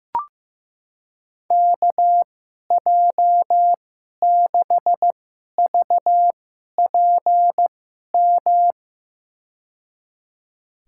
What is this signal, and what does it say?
Morse code, heard as a keyed tone, about 15 wpm, spaced as for 11 wpm, 700 Hz. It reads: KJ6VPM
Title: USA Call Signs - 2x Character Spacing - 15wpm